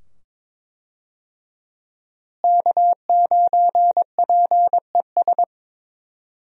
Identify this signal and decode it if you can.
Morse code, heard as a keyed tone, about 22 wpm, 700 Hz. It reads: K9PES
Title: USA Call Signs - 5 Characters - Repeated 2x - 22wpm